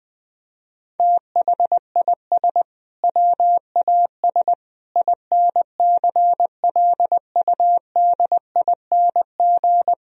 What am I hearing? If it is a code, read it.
Morse code, heard as a keyed tone, about 20 wpm, 700 Hz. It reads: THIS WAS INCLUDING
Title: Sentences from Top 1000 Words - Review Part 4 - 20wpm